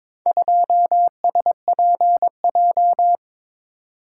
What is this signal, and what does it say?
Morse code, heard as a keyed tone, about 21 wpm, 700 Hz. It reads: 2SPJ
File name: practice / 22wpm / Four Letter-Number Combinations - 22wpm